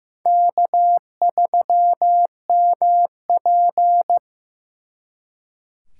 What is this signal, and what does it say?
Morse code, heard as a keyed tone, about 18 wpm, 700 Hz. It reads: K3MP
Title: Speed Racing - USA 1x2 Call Signs 15wpm